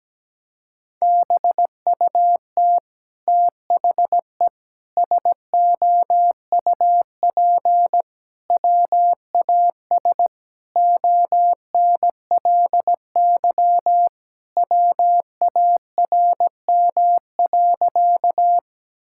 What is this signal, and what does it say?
Morse code, heard as a keyed tone, about 17 wpm, 700 Hz. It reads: BUT THE SOUP WAS ONLY WARM.
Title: Sentences from Top 1000 Words - Review Part 6 - 17wpm